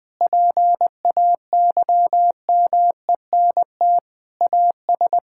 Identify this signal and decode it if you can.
Morse code, heard as a keyed tone, about 20 wpm, 700 Hz. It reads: PAYMENT AS
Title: Sentences from Top 1600 Words 20wpm